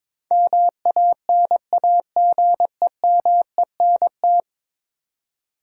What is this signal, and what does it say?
Morse code, heard as a keyed tone, about 22 wpm, 700 Hz. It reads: MANAGEMENT
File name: practice / 22wpm / 10 Letter Words - First 3 Letters Given - 22wpm